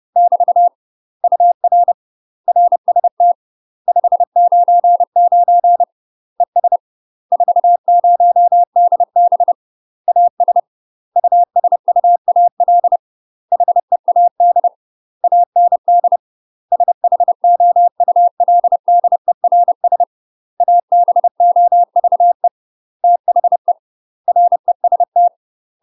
Morse code, heard as a keyed tone, about 30 wpm, 700 Hz. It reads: <BT> UR RST 599 ES 40DB AS USUAL HEAD AND SHOULDERS ABOVE THE REST
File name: UK MASTER QSOs TRAINING / 30WPM / UK MASTER QSOs 0021